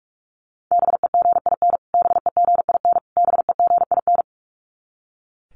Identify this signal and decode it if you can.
Morse code, heard as a keyed tone, about 44 wpm, 700 Hz. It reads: BEGIN BEGIN BEGIN
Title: Top 300 Words Repeated 45wpm